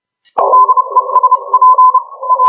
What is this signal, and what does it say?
Morse code, heard as a keyed tone, about 30 wpm, spaced as for 26 wpm, 1020 Hz. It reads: DESP A
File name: HG1R CW SSB MIX